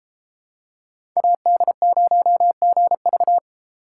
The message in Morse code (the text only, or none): AD0GV